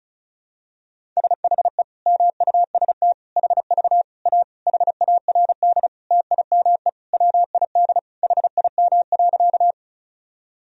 Morse code, heard as a keyed tone, about 34 wpm, 700 Hz. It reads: SHE MUST HV A HARD TIME WID HIM.
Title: Sentences from Top 300 Words and Abbreviations 35wpm